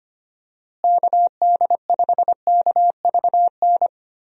Morse code, heard as a keyed tone, about 25 wpm, 700 Hz. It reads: KD5KVN
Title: USA Call Signs - 6 Characters - Repeated 3x - 25wpm